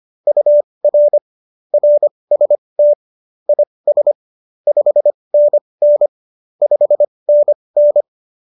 Morse code, heard as a keyed tone, about 25 wpm, spaced as for 20 wpm, 590 Hz. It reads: UR RST IS 5NN 5NN
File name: QSO321 20